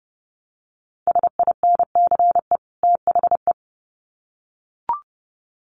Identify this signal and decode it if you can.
Morse code, heard as a keyed tone, about 30 wpm, 700 Hz. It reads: SINCE THE
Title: Sets of 2 Words 30wpm